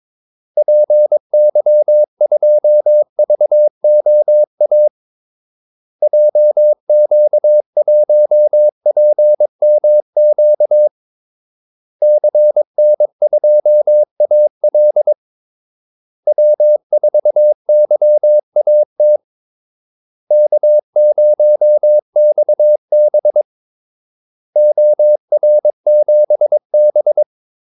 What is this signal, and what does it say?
Morse code, heard as a keyed tone, about 22 wpm, 600 Hz. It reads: PY2VOA JQ1PMQ CN2AL W4YAT K0XB OR7B
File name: Callsigns-22wpm